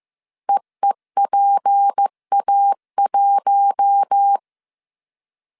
Morse code, heard as a keyed tone, about 15 wpm, 800 Hz. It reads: EEPA1
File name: CW / DET / 1 6 1500C S A 14wpm 2921